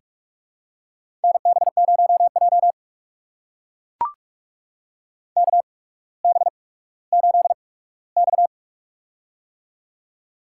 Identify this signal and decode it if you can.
Morse code, heard as a keyed tone, about 44 wpm, spaced as for 19 wpm, 700 Hz. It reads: NB0J KB8X
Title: USA Call Signs - 8x Character Spacing - 45wpm